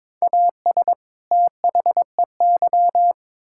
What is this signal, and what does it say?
Morse code, heard as a keyed tone, about 22 wpm, 700 Hz. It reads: AS THEY
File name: Build A Phrase - 2-Word Phrases - 22wpm